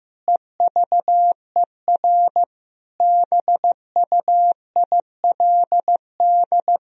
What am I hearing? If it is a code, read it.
Morse code, heard as a keyed tone, about 15 wpm, 700 Hz. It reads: EVER BUILD
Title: Sentences from Top 1000 Words - Review Part 1 - 15wpm